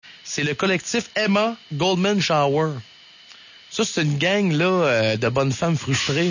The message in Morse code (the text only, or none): none